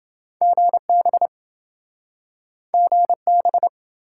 Morse code, heard as a keyed tone, about 28 wpm, 700 Hz. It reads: GB GB